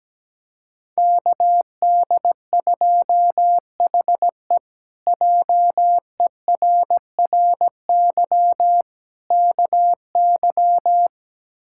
Morse code, heard as a keyed tone, about 17 wpm, 700 Hz. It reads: KD2HE JERRY KY
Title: K1USN SST QSO Practice - 17wpm